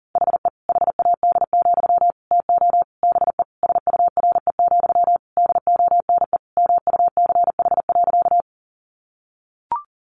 Morse code, heard as a keyed tone, about 40 wpm, 700 Hz. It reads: HE HAD, TO BE SURE, DONE MUCH.
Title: Sentences from Top 500 Words 40wpm